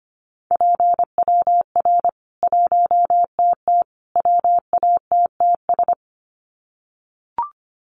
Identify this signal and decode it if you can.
Morse code, heard as a keyed tone, about 25 wpm, 700 Hz. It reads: PWR 1TT WATTS
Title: Common QSO Phrases 25wpm